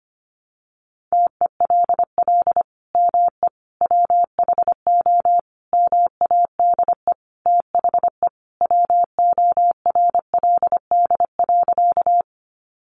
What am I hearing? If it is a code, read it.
Morse code, heard as a keyed tone, about 25 wpm, 700 Hz. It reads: TELL ME WHO MADE THE WORLD.
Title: Sentences from Top 200 Words 25wpm